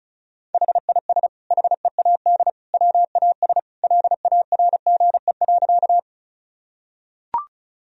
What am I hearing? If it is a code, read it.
Morse code, heard as a keyed tone, about 36 wpm, 700 Hz. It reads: HIS HEAD WAS LARGE.
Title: Sentences from Top 1000 Words - Review Part 1 - 35wpm